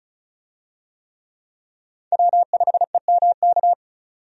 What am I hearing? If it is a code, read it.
Morse code, heard as a keyed tone, about 36 wpm, 700 Hz. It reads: W5EMK
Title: USA Call Signs - 5 Characters - 35wpm